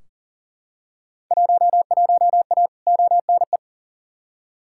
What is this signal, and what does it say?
Morse code, heard as a keyed tone, about 39 wpm, 700 Hz. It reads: 11A ONE
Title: Field Day Exchanges Encore - 40wpm